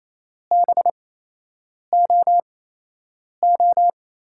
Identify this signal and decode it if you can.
Morse code, heard as a keyed tone, about 28 wpm, spaced as for 8 wpm, 700 Hz. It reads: BOO
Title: USA Call Signs - 6 Characters - 8x Character Spacing - 28wpm